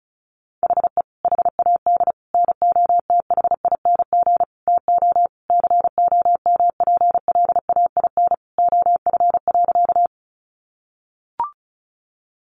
Morse code, heard as a keyed tone, about 34 wpm, 700 Hz. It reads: HE HAD NOTHING TO COMPLAIN OF.
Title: Sentences from Top 1300 Words 35wpm